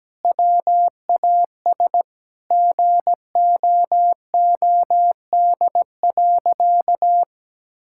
Morse code, heard as a keyed tone, about 17 wpm, 700 Hz. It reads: WAS GOOD.
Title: Sentences from Top 1000 Words - Review Part 5 - 17wpm